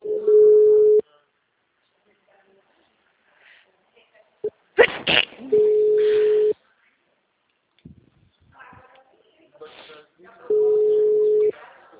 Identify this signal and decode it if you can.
Ringback tone (Europe, 425 Hz), with speech.